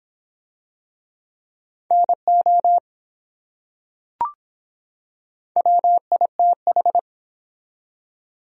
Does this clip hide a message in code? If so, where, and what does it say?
Yes: Morse code from 1.9 to 7.0 s, heard as a keyed tone, about 25 wpm, 700 Hz. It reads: NO WITH